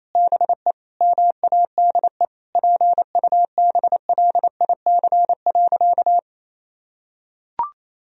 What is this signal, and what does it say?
Morse code, heard as a keyed tone, about 28 wpm, 700 Hz. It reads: BE MADE PUBLIC.
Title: Sentences from Top 1000 Words - Review Part 3 - 28wpm